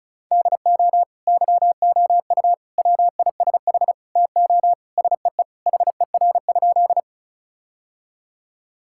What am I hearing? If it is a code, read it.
Morse code, heard as a keyed tone, about 36 wpm, 700 Hz. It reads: DO YOU WISH TO SEE HER?
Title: Sentences from Top 1600 Words 35wpm